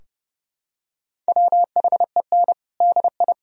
Morse code, heard as a keyed tone, about 30 wpm, 700 Hz. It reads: WHEN DI
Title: Sentences from Top 1000 Words - Review Part 2 - 30wpm